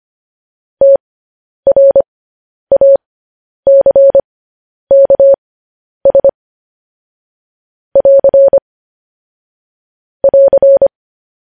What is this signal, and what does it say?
Morse code, heard as a keyed tone, about 25 wpm, spaced as for 10 wpm, 550 Hz. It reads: TRACKS <AR> <AR>